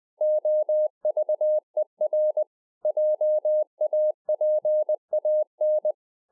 Morse code, heard as a keyed tone, about 20 wpm, 610 Hz. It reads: OVER JAPAN
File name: 2011-04-mp3 / A SMALL TRANCEIVER 3 20WPM